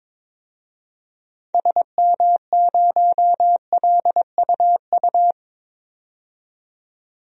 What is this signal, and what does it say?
Morse code, heard as a keyed tone, about 22 wpm, 700 Hz. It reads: SM0LUU
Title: DXCC-100 Call Signs - 6 Characters - 22wpm